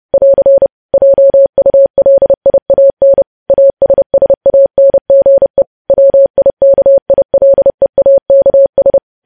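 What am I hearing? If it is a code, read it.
Morse code, heard as a keyed tone, about 30 wpm, 550 Hz. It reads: <AR> JULIAN ASSANGE WIKILEAKS
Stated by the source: Morse code news headlines sent at 30.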